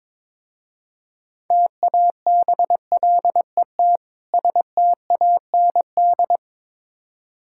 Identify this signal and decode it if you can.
Morse code, heard as a keyed tone, about 22 wpm, 690 Hz. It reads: TABLET STAND
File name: Difficult Words with Hints - 22wpm